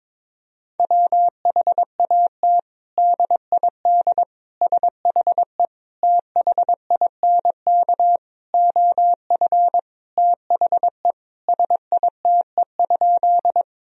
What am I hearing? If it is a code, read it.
Morse code, heard as a keyed tone, about 22 wpm, 700 Hz. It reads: WHAT DID SHE THINK OF THE SITE?
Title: Sentences from Top 1000 Words - Review Part 4 - 22wpm